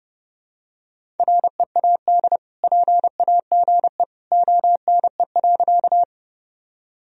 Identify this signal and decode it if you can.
Morse code, heard as a keyed tone, about 30 wpm, 700 Hz. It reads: READ PAGE ONE.
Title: Sentences from Top 1000 Words - Review Part 1 - 30wpm